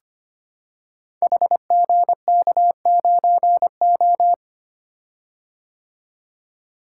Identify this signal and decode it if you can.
Morse code, heard as a keyed tone, about 25 wpm, 700 Hz. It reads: HGK9O
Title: Code Groups - 5 Letter:Numbers - 25wpm